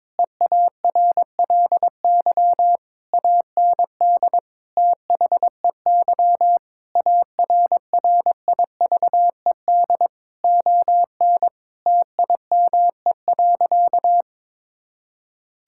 Morse code, heard as a keyed tone, about 22 wpm, 700 Hz. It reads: EARLY AND THEY ARRIVED ON TIME.
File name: practice / 22wpm / Sentences with 2-Word Phrases - 22wpm